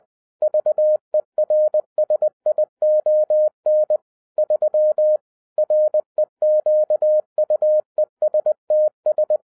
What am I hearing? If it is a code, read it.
Morse code, heard as a keyed tone, about 20 wpm, 610 Hz. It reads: VERSION 3 REQUESTS
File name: 2012-01-mp3 / n15-INVESTIGATION INTO THE POSSIBILITY OF WINNING PREFIX VERSION 3 AWARD-20wpm